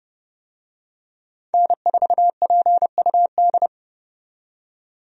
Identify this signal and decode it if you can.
Morse code, heard as a keyed tone, about 29 wpm, 700 Hz. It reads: N4PUD
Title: USA Call Signs - 5 Characters - 30wpm